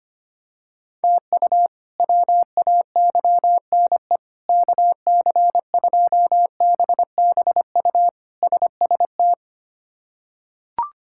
Morse code, heard as a keyed tone, about 25 wpm, 700 Hz. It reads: TU WAYNE KC2BBU SST
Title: K1USN SST QSO Practice - 25wpm